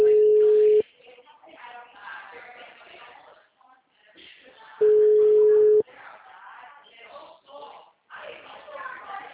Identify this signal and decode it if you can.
Ringback tone (Europe, 425 Hz), with speech.
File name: out- -2011- - - .69837